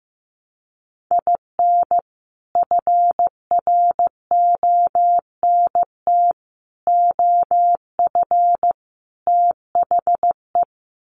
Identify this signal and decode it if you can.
Morse code, heard as a keyed tone, about 15 wpm, 700 Hz. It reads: IN FRONT OF THE
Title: Sets of 4 Words 15wpm